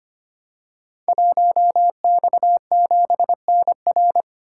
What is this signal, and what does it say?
Morse code, heard as a keyed tone, about 25 wpm, 700 Hz. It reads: 1X7NR
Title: Code Groups - 5 Letter:Numbers - 25wpm